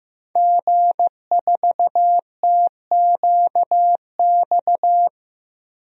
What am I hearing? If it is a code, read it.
Morse code, heard as a keyed tone, about 15 wpm, 700 Hz. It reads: G4TQX
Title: DXCC-100 Call Signs - 5 Characters - 15wpm